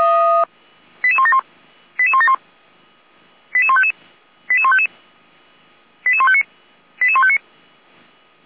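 Five-tone selective call (ZVEI), heard as quick runs of five tones, tones of about 70 ms. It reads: 88171 88171 88155 88155 88159 88159
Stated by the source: ZVEI five-tone selective call received with an SDR.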